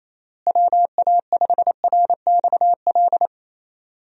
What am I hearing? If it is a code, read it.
Morse code, heard as a keyed tone, about 27 wpm, 700 Hz. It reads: WA5RXL
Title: USA Call Signs - 6 Characters - Repeated 3x - 28wpm